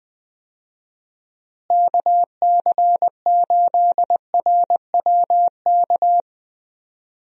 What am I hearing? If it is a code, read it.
Morse code, heard as a keyed tone, about 20 wpm, 700 Hz. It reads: KC8RWK
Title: USA Call Signs - 6 Characters - Repeated 2x - 20wpm